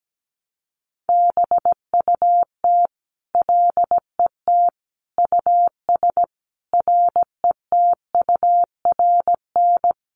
Morse code, heard as a keyed tone, about 17 wpm, 700 Hz. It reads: BUT LET US RETURN
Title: Sentences from Top 800 Words 17wpm